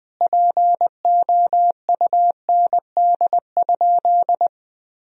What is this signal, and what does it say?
Morse code, heard as a keyed tone, about 20 wpm, 700 Hz. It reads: POUND?
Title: Sentences from Top 1000 Words - Review Part 2 - 20wpm